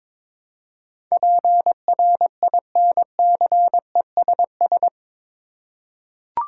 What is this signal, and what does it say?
Morse code, heard as a keyed tone, about 22 wpm, 700 Hz. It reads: PRINCESS
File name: practice / 22wpm / Eight Letter Words - 22wpm